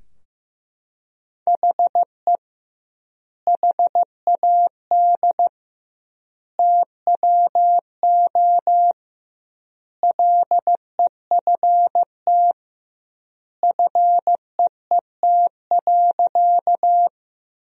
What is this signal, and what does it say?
Morse code, heard as a keyed tone, about 15 wpm, 700 Hz. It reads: HE HAD TWO LEFT FEET.
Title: 2x Word-Spacing - Sentences from Top 300 Words - 15wpm